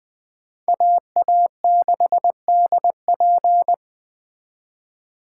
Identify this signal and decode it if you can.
Morse code, heard as a keyed tone, about 20 wpm, 700 Hz. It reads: AA6DP
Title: Speed Racing - USA 2x2 Call Signs 17wpm